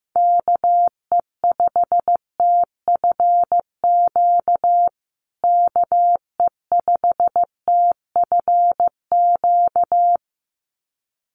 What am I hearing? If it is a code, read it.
Morse code, heard as a keyed tone, about 15 wpm, 700 Hz. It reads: KE5TFQ KE5TFQ
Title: USA Call Signs Repeated 2x 15wpm